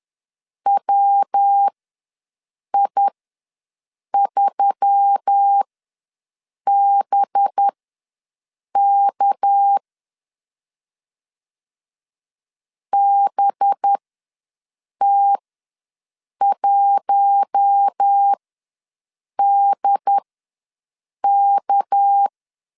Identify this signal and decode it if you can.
Morse code, heard as a keyed tone, about 10 wpm, spaced as for 6 wpm, 800 Hz. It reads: WI3BK BT1DK
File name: LEZIONI / 6 PKX0 / 6 PK ws 10wpm 3745